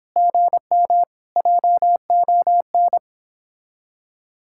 Morse code, heard as a keyed tone, about 29 wpm, 700 Hz. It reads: GM JON